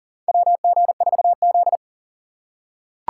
Morse code, heard as a keyed tone, about 40 wpm, 700 Hz. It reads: WG4Z
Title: USA Call Signs - 3x Character Spacing - 40wpm